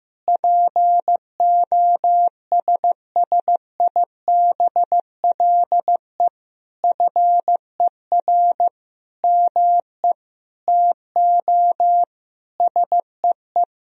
Morse code, heard as a keyed tone, about 15 wpm, 700 Hz. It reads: POSSIBLE FER ME TO SEE
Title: Sentences from Top 500 Words and Abbreviations 15wpm